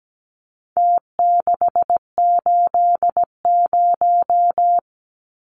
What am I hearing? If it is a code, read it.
Morse code, heard as a keyed tone, about 17 wpm, 700 Hz. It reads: T680
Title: Numbers - Contest Style - 4-Digit 17wpm